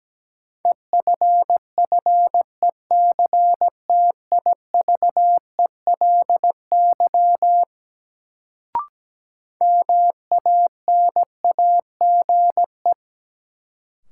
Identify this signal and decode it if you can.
Morse code, heard as a keyed tone, about 17 wpm, 700 Hz. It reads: EFFECTIVELY MANAGE
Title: Sentences from Top 1400 Words - New Word Drill Alt - 17wpm